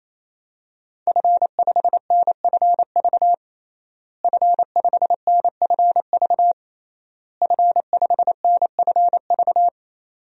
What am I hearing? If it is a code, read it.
Morse code, heard as a keyed tone, about 27 wpm, 700 Hz. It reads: F5NFV F5NFV F5NFV
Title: DXCC-100 Call Signs - 5 Characters - Repeated 3x - 28wpm